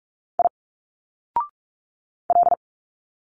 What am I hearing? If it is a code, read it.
Morse code, heard as a keyed tone, about 44 wpm, 700 Hz. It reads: I L
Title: Single Letters 45wpm